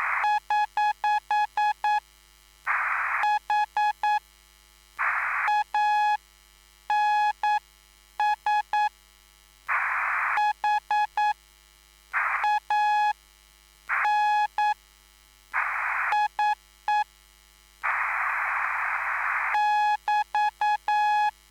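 Morse code, heard as a keyed tone, about 9 wpm, spaced as for 6 wpm, 840 Hz. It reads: <HH> HANS HANIE <BT>
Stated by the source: Live CW QSO between PA2BX and SP3JFK on the 14055 kHz QRS frequency, at a basic 8 wpm.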